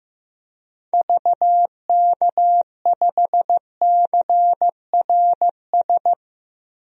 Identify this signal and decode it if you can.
Morse code, heard as a keyed tone, about 15 wpm, 700 Hz. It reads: VK5CRS
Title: DXCC-100 Call Signs - 6 Characters - 15wpm